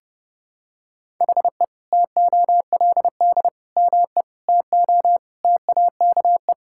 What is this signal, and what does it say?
Morse code, heard as a keyed tone, about 30 wpm, 700 Hz. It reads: HE TOLD ME TO TAKE
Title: Sentences from Top 1000 Words - Review Part 6 - 30wpm